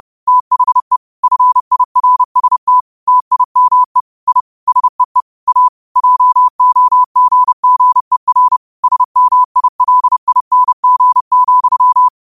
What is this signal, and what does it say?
Morse code, heard as a keyed tone, about 30 wpm, 1000 Hz. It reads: THE FIRST TIME I SEE A JOGGER SMILING,
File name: qotdfiles / 30wpm